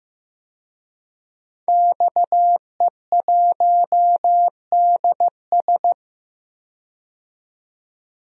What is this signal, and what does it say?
Morse code, heard as a keyed tone, about 15 wpm, 700 Hz. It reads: XE1DS
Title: DXCC-100 Call Signs - 5 Characters - 15wpm